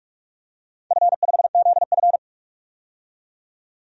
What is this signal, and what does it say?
Morse code, heard as a keyed tone, about 46 wpm, 700 Hz. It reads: F5ZF
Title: DXCC-100 Call Signs - 4 Characters - 45wpm